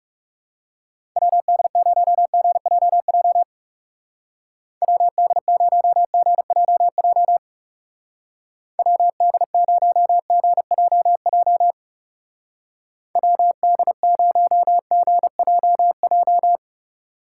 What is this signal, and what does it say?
Morse code, heard as a keyed tone, about 39 wpm, 700 Hz. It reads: WD0GJJ WD0GJJ WD0GJJ WD0GJJ